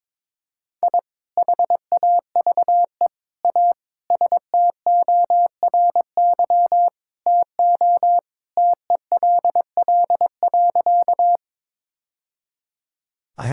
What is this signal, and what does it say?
Morse code, heard as a keyed tone, about 22 wpm, 700 Hz. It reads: I HAVE A STORY TO TELL.
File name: practice / 22wpm / Sentences with 2-Word Phrases - 22wpm